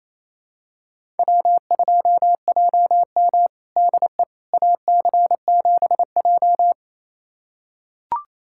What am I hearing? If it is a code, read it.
Morse code, heard as a keyed tone, about 28 wpm, 700 Hz. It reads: W2JM DE AC7J
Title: USA Callsign DE Callsign 28wpm